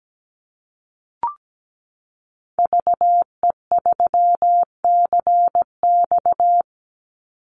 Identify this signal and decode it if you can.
Morse code, heard as a keyed tone, about 17 wpm, 700 Hz. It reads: VE3CX
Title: Call Signs - 17wpm